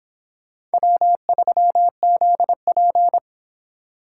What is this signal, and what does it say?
Morse code, heard as a keyed tone, about 31 wpm, 700 Hz. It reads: W3ZP